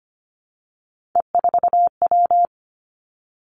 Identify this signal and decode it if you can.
Morse code, heard as a keyed tone, about 25 wpm, 700 Hz. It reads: E4W
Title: Three Challenging Letter-Number Combinations 25wpm